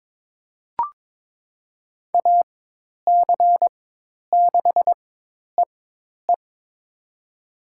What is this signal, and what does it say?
Morse code, heard as a keyed tone, about 22 wpm, spaced as for 10 wpm, 700 Hz. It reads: AC6EE
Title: USA Call Signs - 5 Characters - 4x Character Spacing - 22wpm